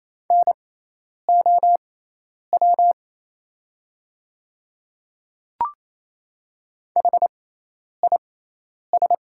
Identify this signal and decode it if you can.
Morse code, heard as a keyed tone, about 27 wpm, spaced as for 10 wpm, 690 Hz. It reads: NOW HIS
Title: 6x Character Spacing - Top 100 Words - 28wpm